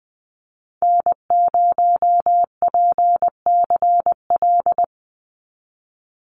Morse code, heard as a keyed tone, about 20 wpm, 700 Hz. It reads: N0PCL
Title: NA SOTA Activator Callsigns 20wpm